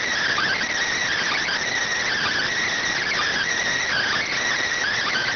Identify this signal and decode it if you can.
Emergency Alert System (SAME) tones, heard as harsh buzzing bursts. It is heard from 0.2 to 4.4 s.